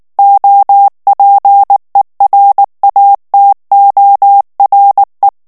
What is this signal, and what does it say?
Morse code, heard as a keyed tone, about 19 wpm, 800 Hz. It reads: OPERATORE
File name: morse mp3 / ITALIEN texte clair 2 vitesse 18 mots minute